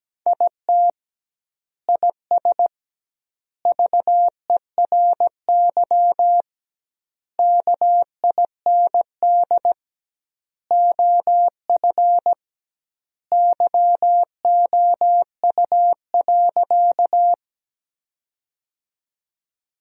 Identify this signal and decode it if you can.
Morse code, heard as a keyed tone, about 17 wpm, 700 Hz. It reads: IT IS VERY KIND OF YOU.
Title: 2x Word-Spacing - Sentences from Top 200 Words - 17wpm